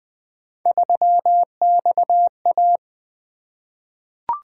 Morse code, heard as a keyed tone, about 20 wpm, 700 Hz. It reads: 3XA
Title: Three Letter-Number Combinations - 20wpm